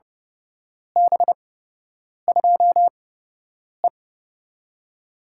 Morse code, heard as a keyed tone, about 31 wpm, spaced as for 8 wpm, 700 Hz. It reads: B2E
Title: USA Call Signs - 4 Characters - 8x Character Spacing - 30wpm